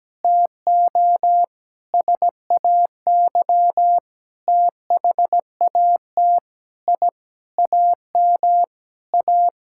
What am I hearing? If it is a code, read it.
Morse code, heard as a keyed tone, about 17 wpm, 700 Hz. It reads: TO SAY THAT I AM A
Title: Sentences from Top 1000 Words - Review Part 6 - 17wpm